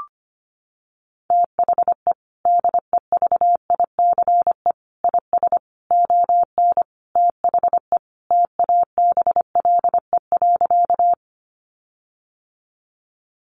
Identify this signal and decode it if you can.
Morse code, heard as a keyed tone, about 25 wpm, 700 Hz. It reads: THE DEVICE IS ON THE TABLE.
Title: Sentences from Top 1500 Words 25wpm